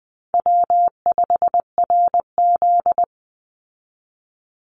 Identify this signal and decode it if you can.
Morse code, heard as a keyed tone, about 19 wpm, 700 Hz. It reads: W5RZ
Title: Easy Call Signs - 20wpm